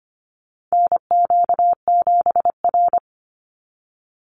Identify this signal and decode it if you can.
Morse code, heard as a keyed tone, about 25 wpm, 700 Hz. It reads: NQ7R
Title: NA SOTA Chaser Callsigns 25wpm